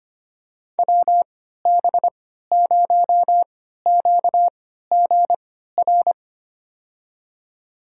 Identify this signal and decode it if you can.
Morse code, heard as a keyed tone, about 25 wpm, spaced as for 14 wpm, 700 Hz. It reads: WB0QGR
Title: USA Call Signs - 6 Characters - 3x Character Spacing - 25wpm